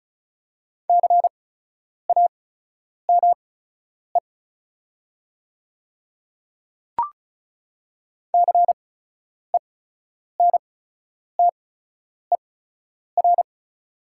Morse code, heard as a keyed tone, about 36 wpm, spaced as for 10 wpm, 700 Hz. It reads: CAME CENTER
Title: 8x Character Spacing - Top 500 Words - 35wpm